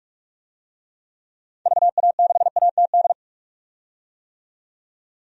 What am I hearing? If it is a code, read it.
Morse code, heard as a keyed tone, about 46 wpm, 700 Hz. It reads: VA6ATD
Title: DXCC-100 Call Signs - 6 Characters - 45wpm